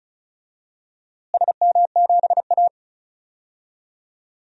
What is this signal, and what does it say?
Morse code, heard as a keyed tone, about 36 wpm, 700 Hz. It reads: SM7A
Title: DXCC-100 Call Signs - 4 Characters - Repeated 3x - 35wpm